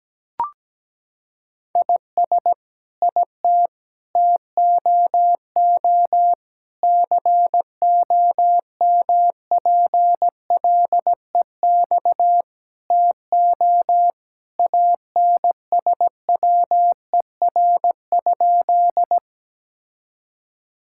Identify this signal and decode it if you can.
Morse code, heard as a keyed tone, about 17 wpm, 700 Hz. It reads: IS IT TOO COMPLEX TO ANSWER?
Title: Sentences from Top 1000 Words - Review Part 6 - 17wpm